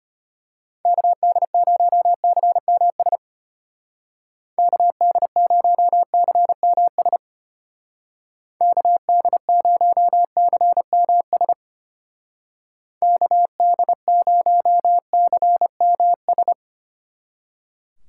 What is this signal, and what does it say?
Morse code, heard as a keyed tone, about 32 wpm, 700 Hz. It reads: KD0CMS KD0CMS KD0CMS KD0CMS